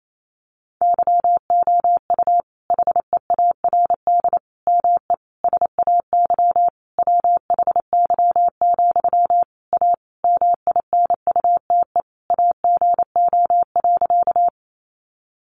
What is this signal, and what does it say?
Morse code, heard as a keyed tone, about 28 wpm, 700 Hz. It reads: YOU HEARD ME SAY WHY, A MINUTE AGO.
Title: Sentences from Top 500 Words 28wpm